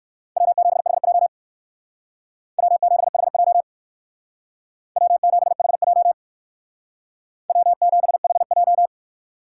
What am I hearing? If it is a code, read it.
Morse code, heard as a keyed tone, about 59 wpm, 700 Hz. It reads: W7HJ W7HJ W7HJ W7HJ